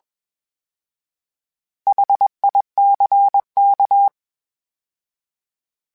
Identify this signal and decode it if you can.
Morse code, heard as a keyed tone, about 21 wpm, 800 Hz. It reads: HICK
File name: H Prefixes / H mp3s